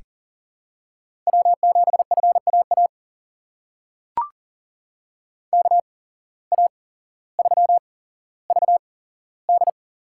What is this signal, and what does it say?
Morse code, heard as a keyed tone, about 40 wpm, spaced as for 17 wpm, 700 Hz. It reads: W7FAA KA3VD
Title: USA Call Signs - 8x Character Spacing - 40wpm